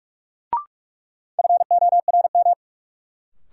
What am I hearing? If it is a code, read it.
Morse code, heard as a keyed tone, about 47 wpm, 700 Hz. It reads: FORM
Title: Sentences from Top 1100 Words - New Word Drill Alt - 45wpm